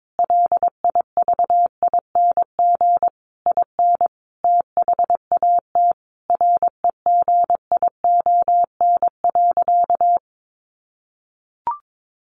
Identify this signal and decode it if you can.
Morse code, heard as a keyed tone, about 22 wpm, 700 Hz. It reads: LIVING IN THAT REGION.
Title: Sentences from Top 1500 Words 22wpm